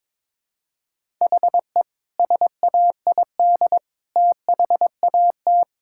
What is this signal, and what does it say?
Morse code, heard as a keyed tone, about 22 wpm, 700 Hz. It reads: HE SAID THAT
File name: practice / 22wpm / Sentences with 2-Word Phrases - 22wpm